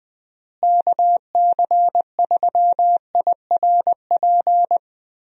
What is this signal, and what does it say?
Morse code, heard as a keyed tone, about 20 wpm, 700 Hz. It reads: KC3IRP
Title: USA Call Signs - 6 Characters - Repeated 3x - 20wpm